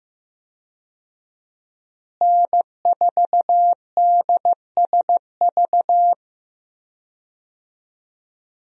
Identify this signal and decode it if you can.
Morse code, heard as a keyed tone, about 14 wpm, 700 Hz. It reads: N4DSV
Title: USA Call Signs - 5 Characters - Repeated 2x - 15wpm